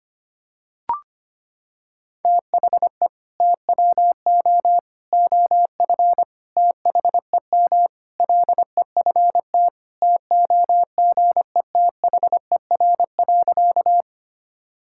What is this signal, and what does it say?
Morse code, heard as a keyed tone, about 25 wpm, 700 Hz. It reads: THE TWO OF THEM LEFT TOGETHER.
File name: practice / 25wpm / Sentences with 4-Word Phrases - 25wpm